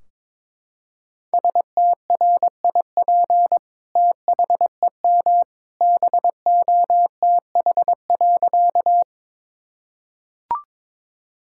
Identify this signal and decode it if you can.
Morse code, heard as a keyed tone, about 22 wpm, 700 Hz. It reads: STRIP THEM BOTH.
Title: Sentences from Top 1600 Words 22wpm